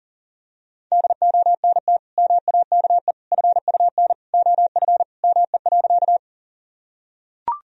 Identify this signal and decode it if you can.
Morse code, heard as a keyed tone, about 40 wpm, 700 Hz. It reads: DONT MAKE FUN OF ME.
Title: Sentences from Top 1000 Words - Review Part 4 - 40wpm